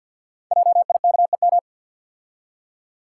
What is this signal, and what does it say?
Morse code, heard as a keyed tone, about 50 wpm, 700 Hz. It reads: JIKEM
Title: Code Groups - 5 Letters - 50wpm